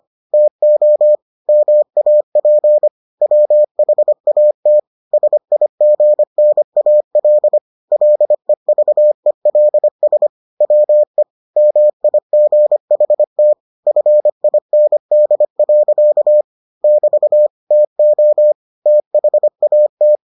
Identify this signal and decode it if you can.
Morse code, heard as a keyed tone, about 25 wpm, 600 Hz. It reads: TO MAP WHAT SIGNAL LEVELS WE MIGHT FIND. <BT> TO THAT